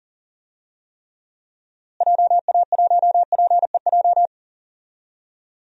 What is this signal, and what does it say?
Morse code, heard as a keyed tone, about 37 wpm, 700 Hz. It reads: JA1PEJ
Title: DXCC-100 Call Signs - 6 Characters - Repeated 3x - 40wpm